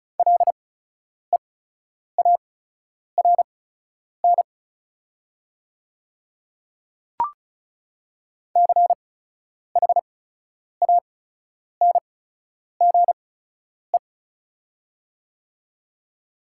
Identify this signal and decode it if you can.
Morse code, heard as a keyed tone, about 36 wpm, spaced as for 10 wpm, 700 Hz. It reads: LEARN CHANGE
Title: 8x Character Spacing - Top 500 Words - 35wpm